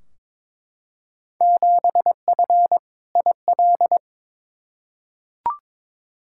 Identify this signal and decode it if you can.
Morse code, heard as a keyed tone, about 21 wpm, 700 Hz. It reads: 7F IL
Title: Field Day Exchanges Encore - 22wpm